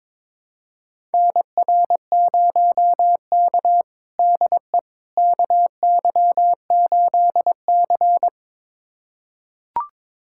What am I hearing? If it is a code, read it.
Morse code, heard as a keyed tone, about 21 wpm, 700 Hz. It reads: NR0K DE KY8C
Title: USA Callsign DE Callsign - 4 Characters - 22wpm